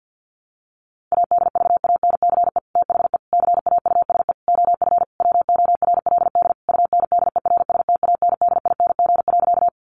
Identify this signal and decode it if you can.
Morse code, heard as a keyed tone, about 50 wpm, 700 Hz. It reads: ADVANCE THE CAUSE OF WORLD UNDERSTANDING.
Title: Sentences from Top 1400 Words 50wpm